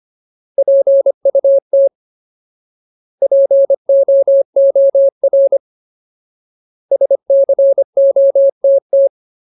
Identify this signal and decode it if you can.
Morse code, heard as a keyed tone, about 25 wpm, 550 Hz. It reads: PUT POOR SCOTT
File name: Ultimate Word List / 25WPM Ultimate